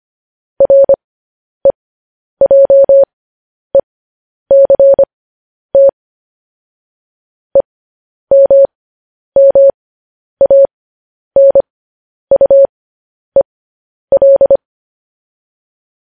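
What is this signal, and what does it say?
Morse code, heard as a keyed tone, about 25 wpm, spaced as for 10 wpm, 550 Hz. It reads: REJECT EMMANUEL
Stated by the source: Morse code transmission of news headlines.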